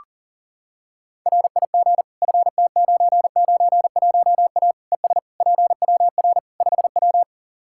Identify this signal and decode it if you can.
Morse code, heard as a keyed tone, about 40 wpm, 700 Hz. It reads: RIG FT991A ES PWR 5W
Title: Practice QSO - Exchange 2 - Info - 40wpm